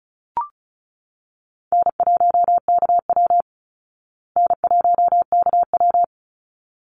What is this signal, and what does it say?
Morse code, heard as a keyed tone, about 35 wpm, 700 Hz. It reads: N1KW N1KW
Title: Easy Call Signs Repeated 2x - 35wpm